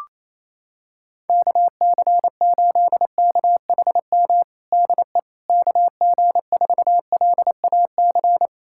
Morse code, heard as a keyed tone, about 28 wpm, 700 Hz. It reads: KC8KHM DE KG4LAC
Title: USA Callsign DE Callsign - 6 Characters - 28wpm